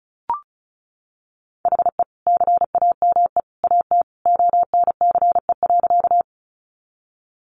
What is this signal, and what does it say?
Morse code, heard as a keyed tone, about 34 wpm, 700 Hz. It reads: HE CAME AT ONCE.
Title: Sentences from Top 300 Words 35wpm